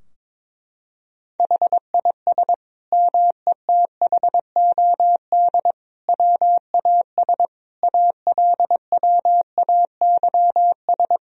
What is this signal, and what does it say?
Morse code, heard as a keyed tone, about 22 wpm, 700 Hz. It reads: HIS METHOD WAS ALWAYS
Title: Sentences from Top 1000 Words - Review Part 3 - 22wpm